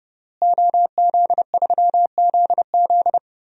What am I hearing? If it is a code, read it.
Morse code, heard as a keyed tone, about 30 wpm, 700 Hz. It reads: OZ3ZZ
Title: DXCC-100 Call Signs - 5 Characters - Repeated 3x - 30wpm